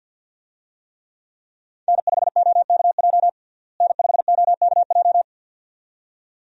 Morse code, heard as a keyed tone, about 51 wpm, 700 Hz. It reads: N5OKJ N5OKJ
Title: USA Call Signs - 5 Characters - Repeated 2x - 50wpm